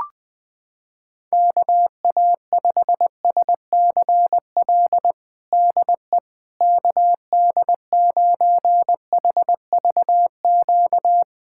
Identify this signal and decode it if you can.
Morse code, heard as a keyed tone, about 20 wpm, 700 Hz. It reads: KA5SCL DE KD9HVQ
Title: USA Callsign DE Callsign - 6 Characters - 20wpm